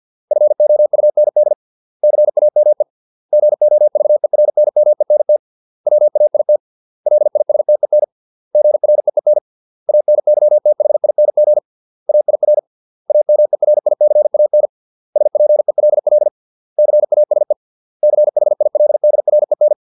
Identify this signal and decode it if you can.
Morse code, heard as a keyed tone, about 50 wpm, 600 Hz. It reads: FOUND CAME GOVERNMENT WAIT LISTEN GREEN ANYTHING AIR AMERICAN SPELL CASE CHILDREN